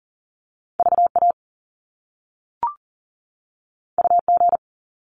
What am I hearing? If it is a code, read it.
Morse code, heard as a keyed tone, about 40 wpm, 700 Hz. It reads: VA UG
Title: Top Two Letter Combinations in Words 40wpm